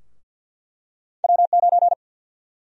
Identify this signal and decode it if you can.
Morse code, heard as a keyed tone, about 53 wpm, 700 Hz. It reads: W9
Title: USA Call Sign Prefixes - 50wpm